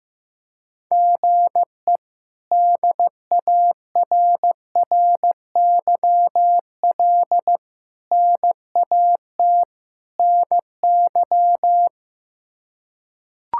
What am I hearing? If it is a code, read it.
Morse code, heard as a keyed tone, about 15 wpm, 700 Hz. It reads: GE DARRYL NAT NY
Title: K1USN SST QSO Practice - 15wpm